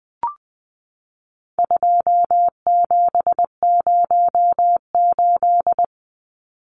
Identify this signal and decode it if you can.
Morse code, heard as a keyed tone, about 20 wpm, 700 Hz. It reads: 2708
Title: Numbers - 4-Digit 20wpm